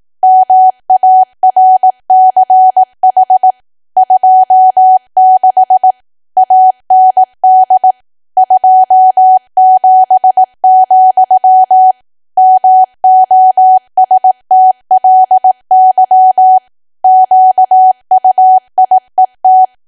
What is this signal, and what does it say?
Morse code, heard as a keyed tone, about 18 wpm, 750 Hz. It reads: MARCH 26 AND 27, MOSTLY QUIET
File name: cwa / W1AW-Level2 / 18WPM